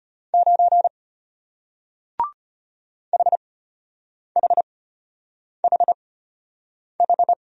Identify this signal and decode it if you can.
Morse code, heard as a keyed tone, about 34 wpm, spaced as for 8 wpm, 700 Hz. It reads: 9 HHHH